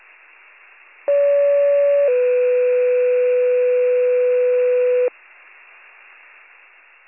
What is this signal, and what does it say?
Two-tone page, heard as a one-second tone, then a three-second tone, about 570 then 500 Hz. It is heard from 1.1 to 5.1 s.